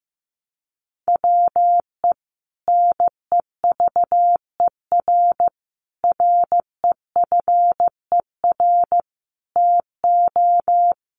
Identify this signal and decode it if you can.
Morse code, heard as a keyed tone, about 15 wpm, 700 Hz. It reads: WE NEVER REFER TO
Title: Sentences from Top 1000 Words 15wpm